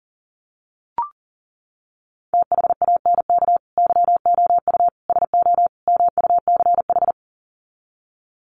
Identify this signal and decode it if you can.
Morse code, heard as a keyed tone, about 40 wpm, 700 Hz. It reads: THANK YOU SO MUCH
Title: Sets of 4 Words 40wpm